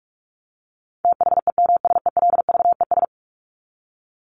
Morse code, heard as a keyed tone, about 44 wpm, 700 Hz. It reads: THEMSELVES
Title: Top 751-1000 Words 45wpm